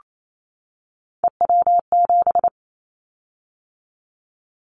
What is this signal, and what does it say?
Morse code, heard as a keyed tone, about 28 wpm, 700 Hz. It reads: EW7
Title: Callsign Prefixes for Top 50 DXCC - 28wpm